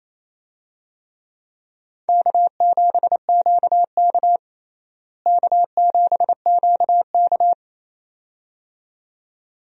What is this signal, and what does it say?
Morse code, heard as a keyed tone, about 28 wpm, 700 Hz. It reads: K7QK K7QK
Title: USA Call Signs - 4 Characters - Repeated 2x - 28wpm